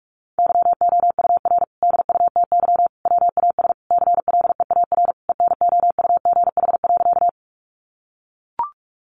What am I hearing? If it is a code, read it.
Morse code, heard as a keyed tone, about 44 wpm, 700 Hz. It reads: YOUR DUTY WAS CLEAR ENOUGH.
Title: Sentences from Top 1500 Words 45wpm